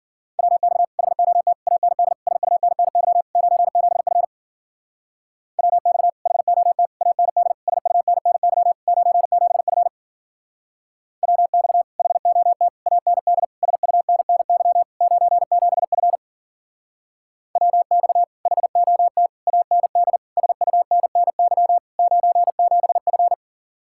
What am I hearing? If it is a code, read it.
Morse code, heard as a keyed tone, about 50 wpm, 700 Hz. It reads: WX HOT AND SUNNY 97F WX HOT AND SUNNY 97F WX HOT AND SUNNY 97F WX HOT AND SUNNY 97F